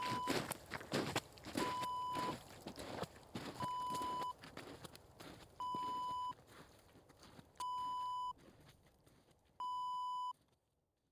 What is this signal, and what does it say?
There is no code.